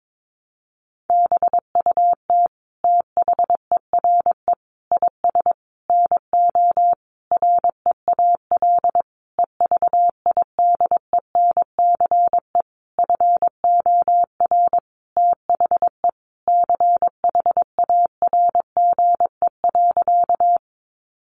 Morse code, heard as a keyed tone, about 22 wpm, 700 Hz. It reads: BUT THERE IS NO REAL EVIDENCE FOR THE CHARGE.
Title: Sentences from Top 1300 Words 22wpm